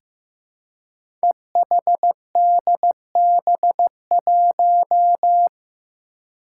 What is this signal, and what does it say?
Morse code, heard as a keyed tone, about 15 wpm, 700 Hz. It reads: EHDB1
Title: Five Challenging Letter-Number Combinations 15wpm